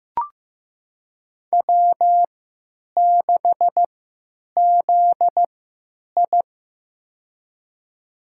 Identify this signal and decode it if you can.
Morse code, heard as a keyed tone, about 15 wpm, spaced as for 9 wpm, 700 Hz. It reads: W6ZI
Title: USA Call Signs - 4 Characters - 3x Character Spacing - 15wpm